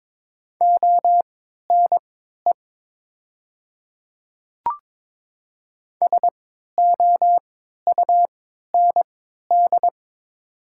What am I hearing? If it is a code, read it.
Morse code, heard as a keyed tone, about 22 wpm, spaced as for 12 wpm, 700 Hz. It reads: ONE SOUND
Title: 3x Character Spacing - Top 100 Words - 22wpm